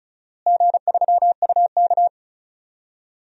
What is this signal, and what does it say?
Morse code, heard as a keyed tone, about 36 wpm, 700 Hz. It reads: G3UK
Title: DXCC-100 Call Signs - 4 Characters - Repeated 3x - 35wpm